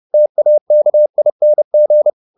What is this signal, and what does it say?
Morse code, heard as a keyed tone, about 30 wpm, 600 Hz. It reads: TAKING
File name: the old man and the sea / the old man and the sea 1 30wpm